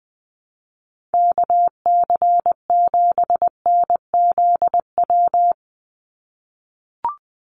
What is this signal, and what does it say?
Morse code, heard as a keyed tone, about 19 wpm, 700 Hz. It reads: KC7NZW
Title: USA Call Signs 20wpm